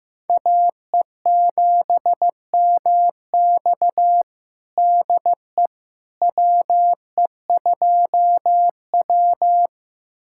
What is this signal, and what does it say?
Morse code, heard as a keyed tone, about 16 wpm, 700 Hz. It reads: AE7MX DE WE2W
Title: USA Callsign DE Callsign 15wpm